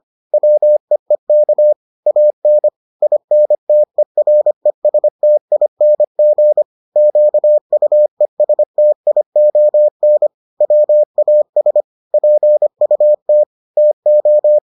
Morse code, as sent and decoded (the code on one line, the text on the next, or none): .-- . -.- / .- -. / .. -. - . .-. ... - .. -. --. / --.- ..- . ... - .. --- -. / .-- .- ... / .--. ..- - / - ---
WEEK AN INTERESTING QUESTION WAS PUT TO